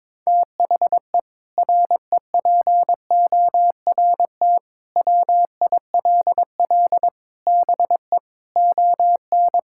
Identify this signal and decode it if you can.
Morse code, heard as a keyed tone, about 22 wpm, 700 Hz. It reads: THE REPORT WILL BE ON
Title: Sentences from Top 1000 Words - Review Part 6 - 22wpm